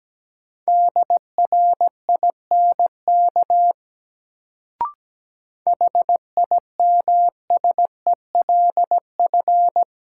Morse code, heard as a keyed tone, about 17 wpm, 700 Hz. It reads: DRINK HIMSELF
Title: Sentences from Top 1500 Words - New Word Drill Alt - 17wpm